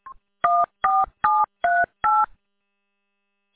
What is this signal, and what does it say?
DTMF, heard as telephone touch tones, tones of about 210 ms. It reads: 14738